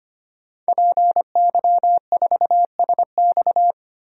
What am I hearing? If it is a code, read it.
Morse code, heard as a keyed tone, about 25 wpm, 700 Hz. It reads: PY4SX
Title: DXCC-100 Call Signs - 5 Characters - 25wpm